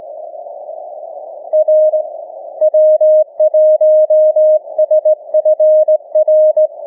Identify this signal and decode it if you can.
Morse code, heard as a keyed tone, about 18 wpm, 620 Hz. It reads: R W1SFR
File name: cw / ssn 30M